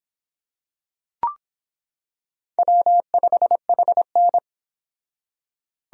Morse code, heard as a keyed tone, about 25 wpm, 700 Hz. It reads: W5HN